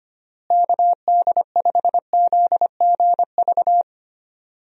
Morse code, heard as a keyed tone, about 25 wpm, 700 Hz. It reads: KD5ZGV
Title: USA Call Signs - 6 Characters - Repeated 2x - 25wpm